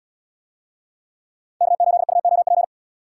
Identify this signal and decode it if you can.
Morse code, heard as a keyed tone, about 75 wpm, 700 Hz. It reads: K8UOP